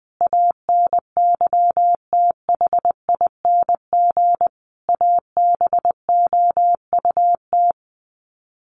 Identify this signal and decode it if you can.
Morse code, heard as a keyed tone, about 20 wpm, 700 Hz. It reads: ANYTHING ABOUT
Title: Sets of 5 Words 20wpm